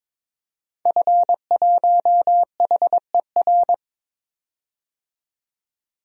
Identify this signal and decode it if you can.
Morse code, heard as a keyed tone, about 22 wpm, 700 Hz. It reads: F1HER
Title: DXCC-100 Call Signs - 5 Characters - 22wpm